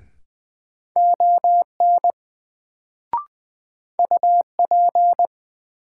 Morse code, heard as a keyed tone, about 20 wpm, 700 Hz. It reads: ON UP
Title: Sets of 2 Words - New Word Drill - 20wpm